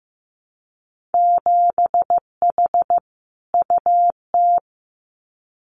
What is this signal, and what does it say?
Morse code, heard as a keyed tone, about 15 wpm, 700 Hz. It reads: 7H UT
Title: Winter Field Day Exchanges Encore 15wpm